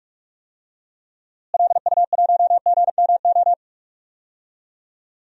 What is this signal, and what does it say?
Morse code, heard as a keyed tone, about 45 wpm, 700 Hz. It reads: LU1GMO
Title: DXCC-100 Call Signs - 6 Characters - Repeated 2x - 45wpm